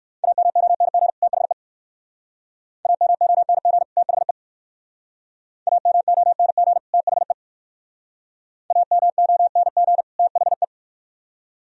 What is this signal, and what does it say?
Morse code, heard as a keyed tone, about 59 wpm, 700 Hz. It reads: AMONG THE AMONG THE AMONG THE AMONG THE